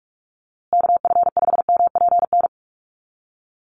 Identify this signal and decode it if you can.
Morse code, heard as a keyed tone, about 44 wpm, 700 Hz. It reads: KF5MPN
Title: USA Call Signs Repeated 2x 45wpm